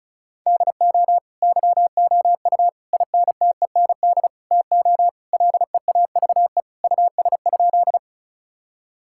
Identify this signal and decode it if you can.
Morse code, heard as a keyed tone, about 36 wpm, 700 Hz. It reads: DO YOU INTEND TO LEAVE US?
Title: Sentences from Top 1600 Words 35wpm